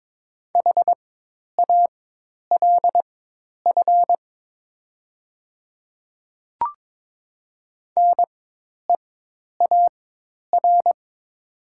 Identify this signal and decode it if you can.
Morse code, heard as a keyed tone, about 22 wpm, spaced as for 10 wpm, 700 Hz. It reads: HALF NEAR
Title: 4x Character Spacing - Top 500 Words - 22wpm